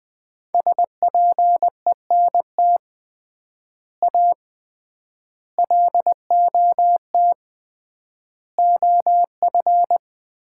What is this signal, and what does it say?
Morse code, heard as a keyed tone, about 20 wpm, 700 Hz. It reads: SPENT A LOT OF
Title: 3x Word-Spacing - Sets of 5 Words - 20wpm